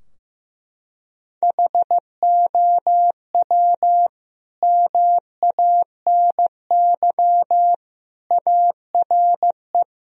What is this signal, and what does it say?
Morse code, heard as a keyed tone, about 14 wpm, 700 Hz. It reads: HOW MANY ARE
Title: Sentences from Top 1000 Words - Review Part 2 - 15wpm